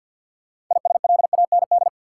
Morse code, heard as a keyed tone, about 52 wpm, 700 Hz. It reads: ISLAND
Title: Sentences from Top 500 Words - New Word Drill - 50wpm